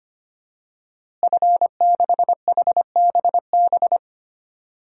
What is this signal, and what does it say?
Morse code, heard as a keyed tone, about 25 wpm, 700 Hz. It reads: F6HBB
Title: DXCC-100 Call Signs - 5 Characters - Repeated 2x - 25wpm